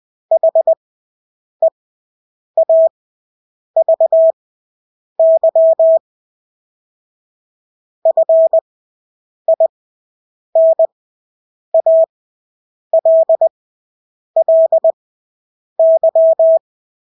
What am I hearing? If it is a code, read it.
Morse code, heard as a keyed tone, about 20 wpm, spaced as for 8 wpm, 650 Hz. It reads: HEAVY FINALLY
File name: Engelske ord 20/8 wpm